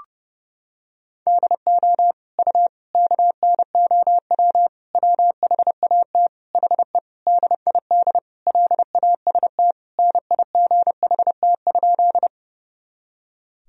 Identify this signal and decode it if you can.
Morse code, heard as a keyed tone, about 32 wpm, 700 Hz. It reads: DO U KNOW WHAT HE DID LAST NIGHT?
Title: Sentences from Top 300 Words and Abbreviations 30wpm